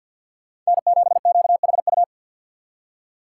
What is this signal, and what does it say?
Morse code, heard as a keyed tone, about 66 wpm, spaced as for 58 wpm, 700 Hz. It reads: N7QHU
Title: Speed Racing - USA 1x3 Call Signs 50wpm